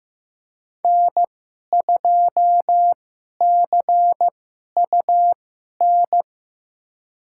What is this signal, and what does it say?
Morse code, heard as a keyed tone, about 15 wpm, spaced as for 11 wpm, 700 Hz. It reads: N2CUN
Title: USA Call Signs - 5 Characters - 2x Character Spacing - 15wpm